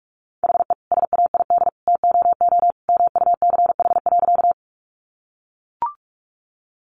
Morse code, heard as a keyed tone, about 44 wpm, 700 Hz. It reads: HE SAID TOO MUCH.
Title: Sentences from Top 200 Words 45wpm